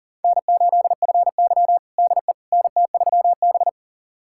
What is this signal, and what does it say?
Morse code, heard as a keyed tone, about 40 wpm, 700 Hz. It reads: N8FY DE NT3B
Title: USA Callsign DE Callsign - 4 Characters - 40wpm